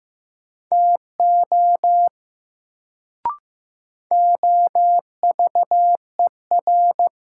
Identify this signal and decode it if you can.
Morse code, heard as a keyed tone, about 16 wpm, 700 Hz. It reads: TO OVER
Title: Sentences from Top 100 Words - New Word Drill - 15wpm